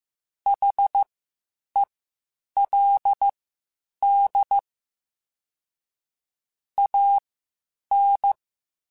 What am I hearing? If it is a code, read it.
Morse code, heard as a keyed tone, about 15 wpm, spaced as for 8 wpm, 800 Hz. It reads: HELD AN